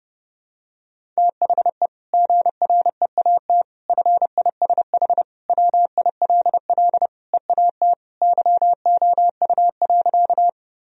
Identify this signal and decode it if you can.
Morse code, heard as a keyed tone, about 30 wpm, 700 Hz. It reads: THE GREAT FISH WILL EAT YOU.
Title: Sentences from Top 1000 Words - Review Part 1 - 30wpm